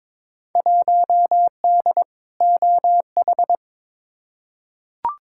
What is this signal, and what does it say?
Morse code, heard as a keyed tone, about 22 wpm, 700 Hz. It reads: 1D OH
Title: Field Day Exchanges - 22wpm